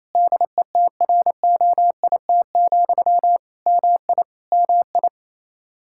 Morse code, heard as a keyed tone, about 27 wpm, 700 Hz. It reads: DETROIT, MI MI
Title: Practice QSO - Exchange 1 - Beginning - 28wpm